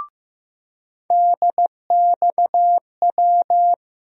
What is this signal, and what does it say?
Morse code, heard as a keyed tone, about 15 wpm, 700 Hz. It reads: DXW
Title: USA Call Sign Postfixes - 3 Letters 15wpm